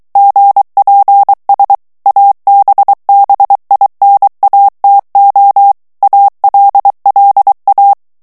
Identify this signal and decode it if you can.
Morse code, heard as a keyed tone, about 23 wpm, 800 Hz. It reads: GPS ABBINATO ALLA
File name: morse mp3 / ITALIEN texte clair 1 vitesse 22 mots minute